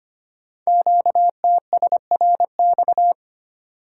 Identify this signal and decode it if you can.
Morse code, heard as a keyed tone, about 25 wpm, 700 Hz. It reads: QTSRX
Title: Code Groups - 5 Letters - 25wpm